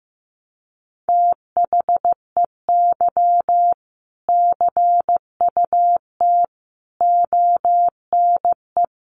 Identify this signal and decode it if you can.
Morse code, heard as a keyed tone, about 15 wpm, 700 Hz. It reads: THEY CUT ONE
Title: Sentences from Top 1400 Words 15wpm